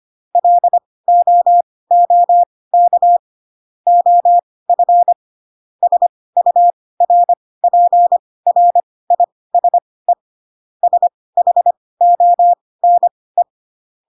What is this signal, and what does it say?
Morse code, heard as a keyed tone, about 25 wpm, spaced as for 18 wpm, 700 Hz. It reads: LOOK OF SURPRISE SHONE
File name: Pinocchio / Pinocchio ch1 12 25 ik1hge